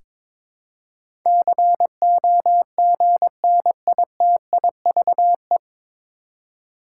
Morse code, heard as a keyed tone, about 22 wpm, 700 Hz. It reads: COGNITIVE
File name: 9 Letter Words - First 3 Letters Given - 22wpm